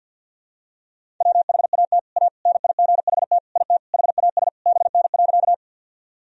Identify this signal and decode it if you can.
Morse code, heard as a keyed tone, about 52 wpm, 700 Hz. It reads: WHAT A NIGHT IT HAS BN.
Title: Sentences from Top 300 Words and Abbreviations 50wpm